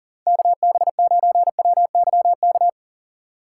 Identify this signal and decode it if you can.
Morse code, heard as a keyed tone, about 50 wpm, 700 Hz. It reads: KB9WYK
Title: Speed Racing - USA 2x3 Call Signs 40wpm